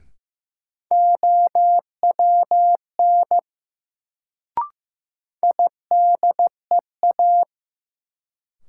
Morse code, heard as a keyed tone, about 15 wpm, 700 Hz. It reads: OWN IDEA
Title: Sentences from Top 300 Words - New Word Drill - 15wpm